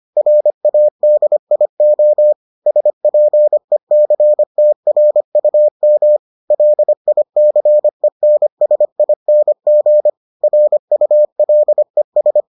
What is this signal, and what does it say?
Morse code, heard as a keyed tone, about 25 wpm, 600 Hz. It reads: RADIO SPECTRUM LICENSING RULES